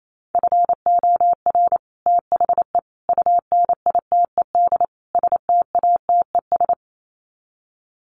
Morse code, heard as a keyed tone, about 28 wpm, 700 Hz. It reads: FOR THE UNITED STATES
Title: Sets of 4 Words 28wpm